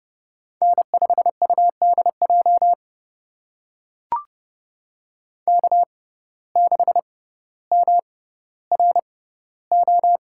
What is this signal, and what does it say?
Morse code, heard as a keyed tone, about 30 wpm, spaced as for 15 wpm, 700 Hz. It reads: N5UDJ K6MRO
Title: USA Call Signs - 6x Character Spacing - 30wpm